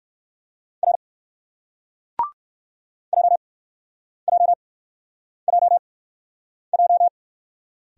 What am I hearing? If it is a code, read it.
Morse code, heard as a keyed tone, about 56 wpm, spaced as for 9 wpm, 700 Hz. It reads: R JJJJ
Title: Speed-Racing Single Letter-Number 45wpm